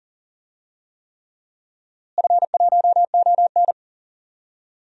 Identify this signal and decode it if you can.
Morse code, heard as a keyed tone, about 40 wpm, 700 Hz. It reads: F1ON
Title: DXCC-100 Call Signs - 4 Characters - Repeated 3x - 40wpm